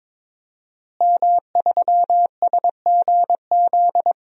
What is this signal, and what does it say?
Morse code, heard as a keyed tone, about 21 wpm, 700 Hz. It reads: M3SGZ
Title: DXCC-100 Call Signs - 5 Characters - Repeated 2x - 22wpm